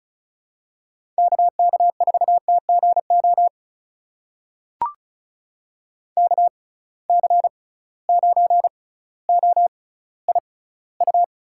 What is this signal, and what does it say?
Morse code, heard as a keyed tone, about 36 wpm, spaced as for 18 wpm, 700 Hz. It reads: KK4TGO KC9OIU
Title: USA Call Signs - 6 Characters - 6x Character Spacing - 35wpm